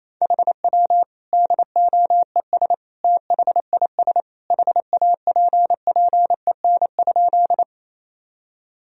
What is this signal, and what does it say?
Morse code, heard as a keyed tone, about 28 wpm, 700 Hz. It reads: HW DOES THIS HAPPEN?
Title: Sentences from Top 400 Words and Abbreviations 28wpm